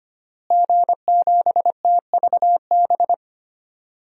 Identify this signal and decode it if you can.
Morse code, heard as a keyed tone, about 25 wpm, 700 Hz. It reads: G7TVB
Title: DXCC-100 Call Signs - 5 Characters - Repeated 3x - 25wpm